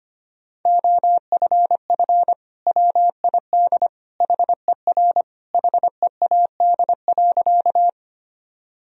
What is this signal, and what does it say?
Morse code, heard as a keyed tone, about 25 wpm, 700 Hz. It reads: OFF WID HER HEAD.
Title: Sentences from Top 300 Words and Abbreviations 25wpm